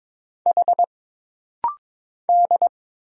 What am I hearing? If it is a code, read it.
Morse code, heard as a keyed tone, about 22 wpm, 700 Hz. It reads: H D
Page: This rapid-fire timing is geared to push your ICR (Instant Character Recognition) skills.